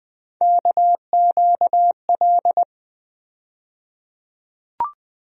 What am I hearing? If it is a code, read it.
Morse code, heard as a keyed tone, about 20 wpm, 700 Hz. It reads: KQL
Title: USA Call Sign Postfixes - 3 Letters 20wpm